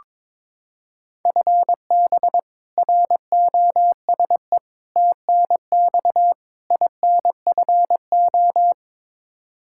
Morse code, heard as a keyed tone, about 21 wpm, 700 Hz. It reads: FB ROSE TNX INFO
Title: Practice QSO - Exchange 3 - More Info - 22wpm